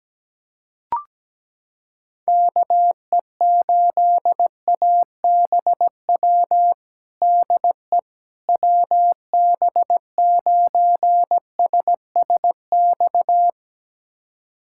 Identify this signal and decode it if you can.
Morse code, heard as a keyed tone, about 17 wpm, 700 Hz. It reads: KE8ABW DE WB9SSX
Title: USA Callsign DE Callsign - 6 Characters - 17wpm